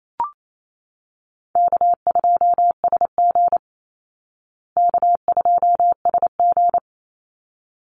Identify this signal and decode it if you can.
Morse code, heard as a keyed tone, about 28 wpm, 700 Hz. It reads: K2SG K2SG
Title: Call Signs Repeated 3x - 28wpm